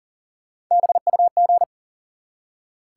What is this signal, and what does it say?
Morse code, heard as a keyed tone, about 40 wpm, 700 Hz. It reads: BUG
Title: Three Letter Words - 40wpm